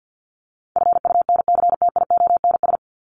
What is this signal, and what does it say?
Morse code, heard as a keyed tone, about 50 wpm, 700 Hz. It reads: FUNCTIONS
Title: ES:S Suffix - Top 100 Words with Base Words 50wpm